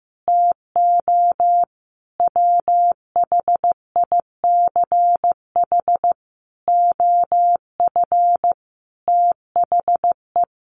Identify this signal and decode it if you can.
Morse code, heard as a keyed tone, about 15 wpm, 700 Hz. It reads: TO WHICH OF THE
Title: Sentences from Top 1200 Words 15wpm